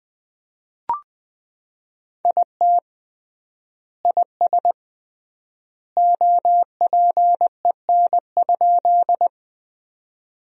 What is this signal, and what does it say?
Morse code, heard as a keyed tone, about 21 wpm, 700 Hz. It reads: IT IS OPEN?
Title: 3x Word-Spacing - Sentences from Top 300 Words - 20wpm